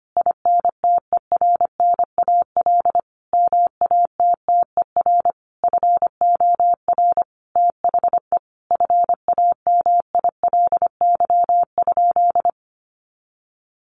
Morse code, heard as a keyed tone, about 25 wpm, 700 Hz. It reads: INTERNAL MATTER FOR THE FAMILY?
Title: Sentences from Top 1400 Words 25wpm